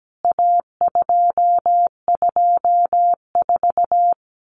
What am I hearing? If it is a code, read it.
Morse code, heard as a keyed tone, about 17 wpm, 700 Hz. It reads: A224
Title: CWT Exchange 17wpm